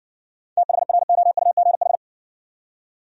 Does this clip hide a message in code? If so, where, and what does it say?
Yes: Morse code from 0.6 to 2.0 s, heard as a keyed tone, about 60 wpm, 700 Hz. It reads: THROUGH